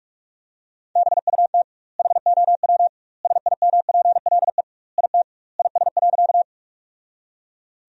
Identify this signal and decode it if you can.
Morse code, heard as a keyed tone, about 42 wpm, 700 Hz. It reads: BUT HOW SIMPLE IT IS.
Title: 2x Word-Spacing - Sentences from Top 400 Words - 45wpm